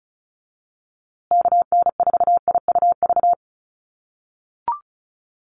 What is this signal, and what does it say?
Morse code, heard as a keyed tone, about 34 wpm, 700 Hz. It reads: KN4IUV
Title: USA Call Signs Repeated 3x 35wpm